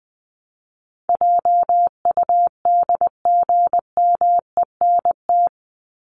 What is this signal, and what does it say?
Morse code, heard as a keyed tone, about 20 wpm, 700 Hz. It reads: JUDGMENT
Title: Top 1751-2000 Words 20wpm